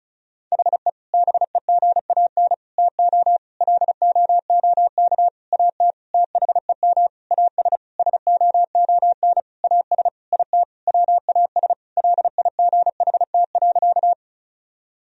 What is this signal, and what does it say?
Morse code, heard as a keyed tone, about 34 wpm, 700 Hz. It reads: HE BEGAN TO LOOK AT THEM AS SOON AS IT WAS LIGHT.
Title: Sentences from Top 1000 Words - Review Part 1 - 35wpm